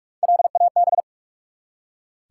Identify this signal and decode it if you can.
Morse code, heard as a keyed tone, about 44 wpm, 700 Hz. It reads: LAB
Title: Three Letter Words - 45wpm